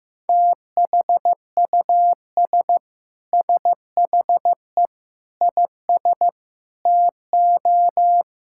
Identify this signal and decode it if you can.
Morse code, heard as a keyed tone, about 15 wpm, 700 Hz. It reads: THUS SHE IS TO
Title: Sentences from Top 1000 Words - Review Part 5 - 15wpm